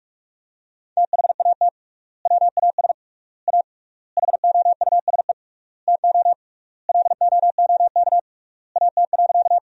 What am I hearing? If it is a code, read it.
Morse code, heard as a keyed tone, about 43 wpm, 700 Hz. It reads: THAT WAS A HOUSE TO LOOK AT.
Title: 3x Word-Spacing - Sentences from Top 200 Words - 45wpm